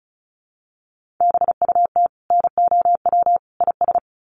Morse code, heard as a keyed tone, about 34 wpm, 700 Hz. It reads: BUT NOW IS
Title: Sentences from Top 1500 Words 35wpm